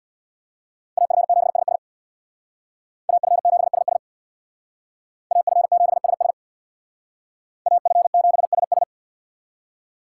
Morse code, heard as a keyed tone, about 62 wpm, 700 Hz. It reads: AF7SS AF7SS AF7SS AF7SS
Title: Speed Racing - USA 2x2 Call Signs 50wpm